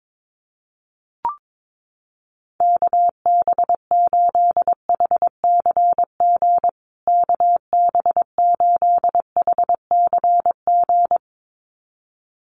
Morse code, heard as a keyed tone, about 22 wpm, 700 Hz. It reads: KB8HCG KB8HCG
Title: USA Call Signs Repeated 2x 22wpm